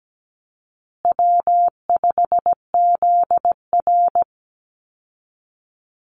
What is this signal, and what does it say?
Morse code, heard as a keyed tone, about 17 wpm, 700 Hz. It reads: W5ZR
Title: Easy Call Signs Repeated 3x - 17wpm